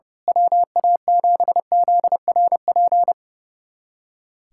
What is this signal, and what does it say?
Morse code, heard as a keyed tone, about 28 wpm, 700 Hz. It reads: WA7ZRP
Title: USA Callsign DE Callsign - 6 Characters - 30wpm